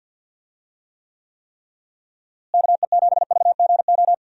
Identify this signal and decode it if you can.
Morse code, heard as a keyed tone, about 50 wpm, 700 Hz. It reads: KE7VGO
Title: USA Call Signs - 6 Characters - Repeated 3x - 50wpm